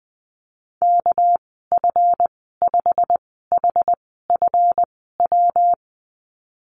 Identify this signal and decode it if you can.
Morse code, heard as a keyed tone, about 20 wpm, spaced as for 15 wpm, 700 Hz. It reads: KF5HFW
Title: 207 Lesson 33g - Letter j - Call Signs 20wpm